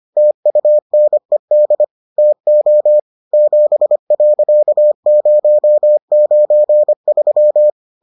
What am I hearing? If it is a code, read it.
Morse code, heard as a keyed tone, about 25 wpm, 600 Hz. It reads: TUNED TO 7.093